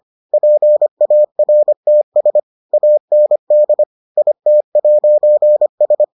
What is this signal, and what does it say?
Morse code, heard as a keyed tone, about 25 wpm, 600 Hz. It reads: PARTS AND IT'S